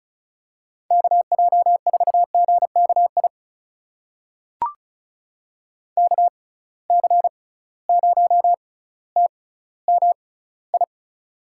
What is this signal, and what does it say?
Morse code, heard as a keyed tone, about 36 wpm, spaced as for 18 wpm, 700 Hz. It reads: KJ4GKI KC0TMI
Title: USA Call Signs - 6 Characters - 6x Character Spacing - 35wpm